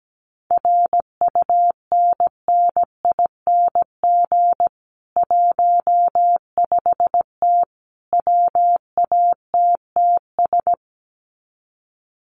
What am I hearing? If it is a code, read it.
Morse code, heard as a keyed tone, about 17 wpm, 700 Hz. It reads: RUNNING 15T WATTS
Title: Common QSO Phrases 17wpm